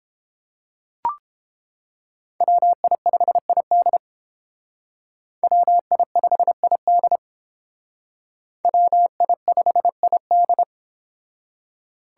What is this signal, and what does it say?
Morse code, heard as a keyed tone, about 30 wpm, 700 Hz. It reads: WI5ID WI5ID WI5ID